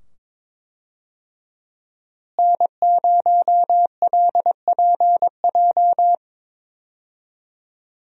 Morse code, heard as a keyed tone, about 22 wpm, 700 Hz. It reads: N0LPJ
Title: USA Call Signs - 5 Characters - Repeated 2x - 22wpm